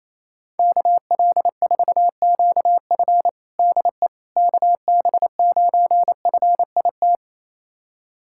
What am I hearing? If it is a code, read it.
Morse code, heard as a keyed tone, about 28 wpm, 700 Hz. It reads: KL4QF DE KB9FIT
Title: USA Callsign DE Callsign 28wpm